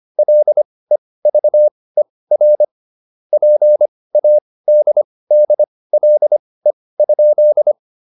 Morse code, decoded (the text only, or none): LEVER PADDLE?